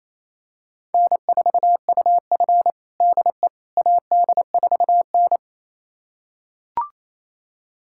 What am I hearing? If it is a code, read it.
Morse code, heard as a keyed tone, about 28 wpm, 700 Hz. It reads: N4UF DE AD4N
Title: USA Callsign DE Callsign - 4 Characters - 28wpm